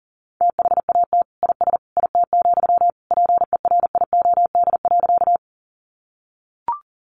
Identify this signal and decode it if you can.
Morse code, heard as a keyed tone, about 40 wpm, 700 Hz. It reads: THAT IS IT, PERIOD.
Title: Sentences from Top 800 Words 40wpm